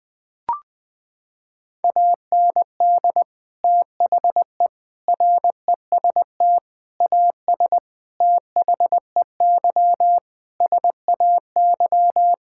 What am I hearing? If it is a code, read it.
Morse code, heard as a keyed tone, about 21 wpm, 700 Hz. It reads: AND THE REST AS THEY SAY
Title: Sets of 8 Words - 20wpm